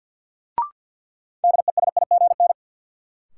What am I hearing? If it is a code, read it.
Morse code, heard as a keyed tone, about 51 wpm, 700 Hz. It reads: DESIGN
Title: Sentences from Top 800 Words - New Word Drill - 50wpm